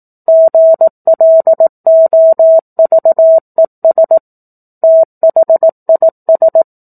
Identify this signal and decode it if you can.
Morse code, heard as a keyed tone, about 18 wpm, 650 Hz. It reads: GLOVES THIS
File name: 04-Chapter-04 18 WPM